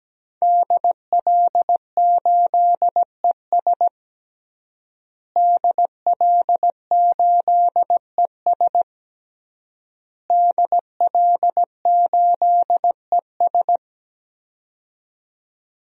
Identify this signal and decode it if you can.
Morse code, heard as a keyed tone, about 17 wpm, 700 Hz. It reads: DL8ES DL8ES DL8ES
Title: DXCC-100 Call Signs - 5 Characters - Repeated 3x - 17wpm